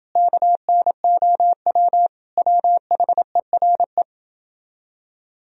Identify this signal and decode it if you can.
Morse code, heard as a keyed tone, about 31 wpm, 700 Hz. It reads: KNOW WHERE